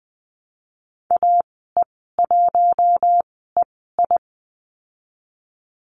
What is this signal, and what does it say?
Morse code, heard as a keyed tone, about 20 wpm, spaced as for 14 wpm, 700 Hz. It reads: AE1EI
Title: 048 Lesson 9g - Number 4 - Call Signs 20wpm